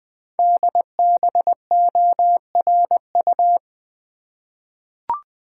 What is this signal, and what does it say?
Morse code, heard as a keyed tone, about 20 wpm, 700 Hz. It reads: DBORU
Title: Code Groups - 5 Letters - 20wpm